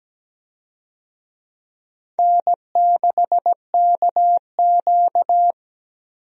Morse code, heard as a keyed tone, about 17 wpm, 700 Hz. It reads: N6KQ
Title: USA Call Signs - 4 Characters - 17wpm